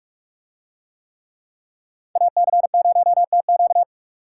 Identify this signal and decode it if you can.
Morse code, heard as a keyed tone, about 46 wpm, 700 Hz. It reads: AC0TQ